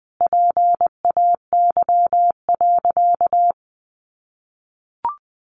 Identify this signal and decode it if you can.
Morse code, heard as a keyed tone, about 20 wpm, 700 Hz. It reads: PAY.
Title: Sentences from Top 1500 Words 20wpm